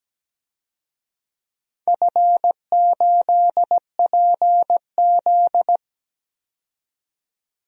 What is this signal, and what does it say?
Morse code, heard as a keyed tone, about 17 wpm, 700 Hz. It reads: F8PZ
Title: DXCC-100 Call Signs - 4 Characters - 17wpm